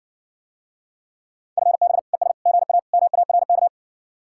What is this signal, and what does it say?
Morse code, heard as a keyed tone, about 60 wpm, 700 Hz. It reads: FB ES GA MARK